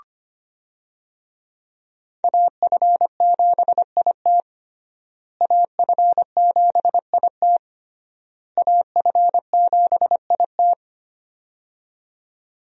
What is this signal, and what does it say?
Morse code, heard as a keyed tone, about 25 wpm, 700 Hz. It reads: AF7IT AF7IT AF7IT
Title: USA Call Signs - 5 Characters - Repeated 3x - 25wpm